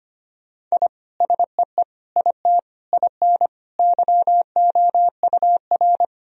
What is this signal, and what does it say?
Morse code, heard as a keyed tone, about 25 wpm, 700 Hz. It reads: I SEE IT IN YOUR
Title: Sentences from Top 1000 Words - Review Part 1 - 25wpm